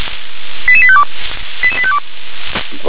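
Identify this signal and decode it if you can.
Five-tone selective call (ZVEI), heard as quick runs of five tones, tones of about 70 ms. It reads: 88852 88852
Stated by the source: Five-tone ZVEI selective call 88722, received by SDR.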